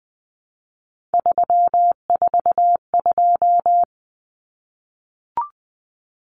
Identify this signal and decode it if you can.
Morse code, heard as a keyed tone, about 20 wpm, 700 Hz. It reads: 342
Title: Numbers - 3-Digit 20wpm